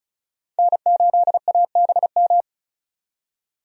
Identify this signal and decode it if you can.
Morse code, heard as a keyed tone, about 36 wpm, 700 Hz. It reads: N8ABM
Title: USA Call Signs - 5 Characters - 35wpm